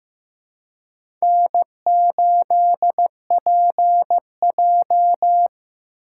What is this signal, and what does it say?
Morse code, heard as a keyed tone, about 15 wpm, 700 Hz. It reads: N8PJ
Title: USA Call Signs - 3x Character Spacing - 15wpm